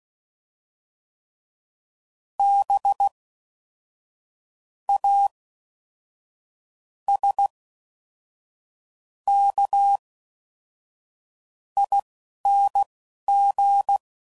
Morse code, heard as a keyed tone, about 16 wpm, spaced as for 11 wpm, 800 Hz. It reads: B A S K ING
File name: Suffixes / Special INGs / Fast INGs